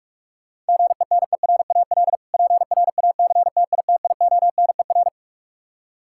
Morse code, heard as a keyed tone, about 47 wpm, 700 Hz. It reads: GENERAL PRACTITIONER
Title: Practice QSO - Exchange 3 - More Info - 45wpm